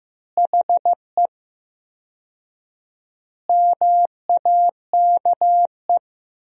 Morse code, heard as a keyed tone, about 15 wpm, 700 Hz. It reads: HE MAKE
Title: 4x Word-Spacing - Sentences from Top 400 Words - 15wpm